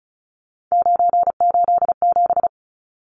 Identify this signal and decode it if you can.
Morse code, heard as a keyed tone, about 34 wpm, 700 Hz. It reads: 987
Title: Numbers - 3-Digit 35wpm